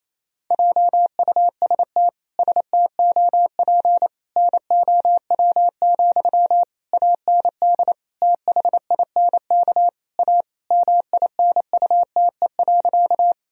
Morse code, heard as a keyed tone, about 28 wpm, 700 Hz. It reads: JUST STOP NOW, AND THINK A MINUTE.
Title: Sentences from Top 1000 Words - Review Part 1 - 28wpm